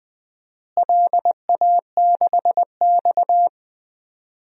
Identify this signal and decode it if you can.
Morse code, heard as a keyed tone, about 20 wpm, 700 Hz. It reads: LA6X
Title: DXCC-100 Call Signs - 4 Characters - Repeated 2x - 20wpm